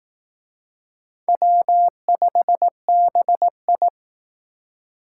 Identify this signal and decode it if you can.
Morse code, heard as a keyed tone, about 18 wpm, 700 Hz. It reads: W5BI
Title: Speed Racing - USA Call Signs 15wpm